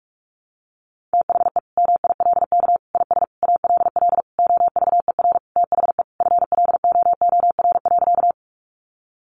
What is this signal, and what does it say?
Morse code, heard as a keyed tone, about 44 wpm, 700 Hz. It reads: THE MILK IS ALL OVER THE FLOOR.
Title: Additional Easy Phrases 45wpm